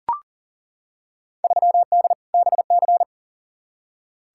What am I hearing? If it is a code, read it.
Morse code, heard as a keyed tone, about 40 wpm, 700 Hz. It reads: 3D BC
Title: Field Day Exchanges Encore - 40wpm